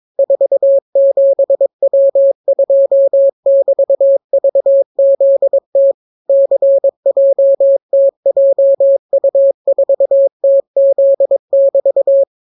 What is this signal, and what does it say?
Morse code, heard as a keyed tone, about 22 wpm, 550 Hz. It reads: 47W2<BT>VZT CJTJU4TZ<BT>